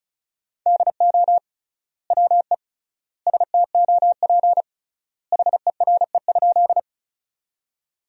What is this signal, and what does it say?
Morse code, heard as a keyed tone, about 36 wpm, 700 Hz. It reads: DO WE STOP HERE?
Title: 3x Word-Spacing - Sentences from Top 300 Words - 35wpm